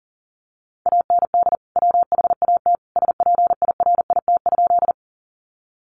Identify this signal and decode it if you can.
Morse code, heard as a keyed tone, about 40 wpm, 700 Hz. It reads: AND WHAT SPIRIT?
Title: Sentences from Top 1400 Words 40wpm